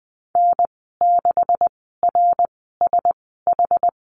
Morse code, heard as a keyed tone, about 20 wpm, spaced as for 14 wpm, 700 Hz. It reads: N6RSH
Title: 140 Lesson 23g - Letter f - Call Signs 20wpm